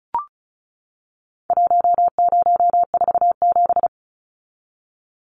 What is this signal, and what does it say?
Morse code, heard as a keyed tone, about 34 wpm, 700 Hz. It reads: 1047
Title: Numbers - Contest Style - 4-Digit 35wpm